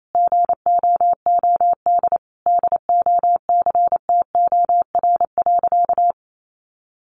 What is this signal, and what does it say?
Morse code, heard as a keyed tone, about 28 wpm, 700 Hz. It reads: GOOD DOCTOR.
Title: Sentences from Top 1500 Words 28wpm